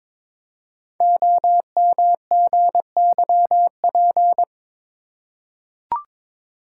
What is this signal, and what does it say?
Morse code, heard as a keyed tone, about 22 wpm, 700 Hz. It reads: OMGYP
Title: Code Groups - 5 Letters - 22wpm